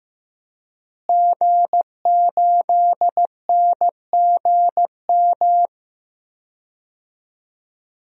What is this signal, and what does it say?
Morse code, heard as a keyed tone, about 15 wpm, 700 Hz. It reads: G8NGM
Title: DXCC-100 Call Signs - 5 Characters - Repeated 3x - 15wpm